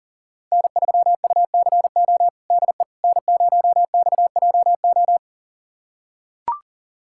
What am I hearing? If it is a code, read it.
Morse code, heard as a keyed tone, about 37 wpm, 700 Hz. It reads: N3UCO DE N0XJO
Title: USA Callsign DE Callsign - 5 Characters - 40wpm